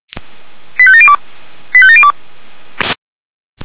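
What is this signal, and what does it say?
Five-tone selective call (ZVEI), heard as quick runs of five tones, tones of about 70 ms. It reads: 85702 85702
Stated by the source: Five-tone ZVEI selective call received on SDR, fire brigade alerting.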